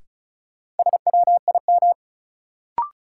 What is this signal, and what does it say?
Morse code, heard as a keyed tone, about 36 wpm, 700 Hz. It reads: SWIM
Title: Sentences from Top 1000 Words - Review Part 6 - New Word Drill - 35wpm